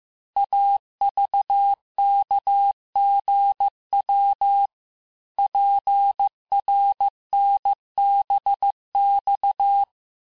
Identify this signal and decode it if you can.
Morse code, heard as a keyed tone, about 15 wpm, 800 Hz. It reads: AVKGW PRNBX
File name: cours-du-lundi files / cours-1585-texte-chiffre-vitesse-14mpm